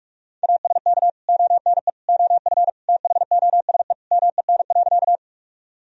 Morse code, heard as a keyed tone, about 41 wpm, 700 Hz. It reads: ASK ONE OF THOSE MEN.
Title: Sentences from Top 1000 Words - Review Part 1 - 45wpm